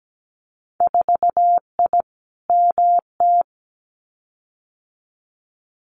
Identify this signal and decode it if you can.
Morse code, heard as a keyed tone, about 17 wpm, 700 Hz. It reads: 4I MT
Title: Winter Field Day Exchanges Encore 17wpm